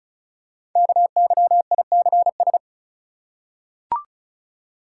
Morse code, heard as a keyed tone, about 36 wpm, 700 Hz. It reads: KYICS
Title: Code Groups - 5 Letters - 35wpm